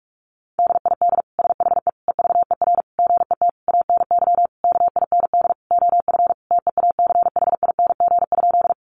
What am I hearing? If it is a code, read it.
Morse code, heard as a keyed tone, about 44 wpm, 700 Hz. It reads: DID SHE EVER GET ANY KIND OF TEACHING?
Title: Sentences from Top 1100 Words 45wpm